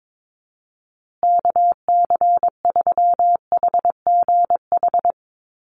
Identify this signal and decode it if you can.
Morse code, heard as a keyed tone, about 22 wpm, 700 Hz. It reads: KC3HGH
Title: USA Call Signs Repeated 2x 22wpm